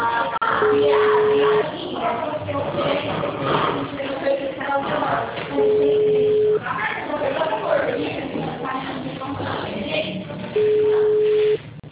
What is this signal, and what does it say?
Ringback tone (Europe, 425 Hz), with speech.
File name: out- -2015- - - .14